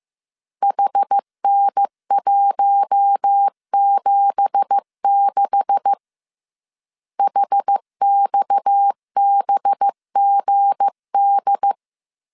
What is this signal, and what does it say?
Morse code, heard as a keyed tone, about 15 wpm, 800 Hz. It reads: HN176 HXBGD
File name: CW / DET / 1 9 1500C S C 14wpm 3245